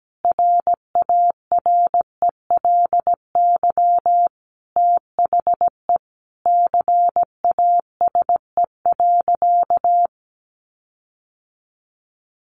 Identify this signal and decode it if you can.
Morse code, heard as a keyed tone, about 17 wpm, 700 Hz. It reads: RARELY THE CASE.
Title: Sentences from Top 1400 Words 17wpm